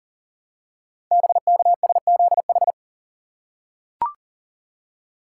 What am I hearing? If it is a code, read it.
Morse code, heard as a keyed tone, about 40 wpm, 700 Hz. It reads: BKSZH
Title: Code Groups - 5 Letters - 40wpm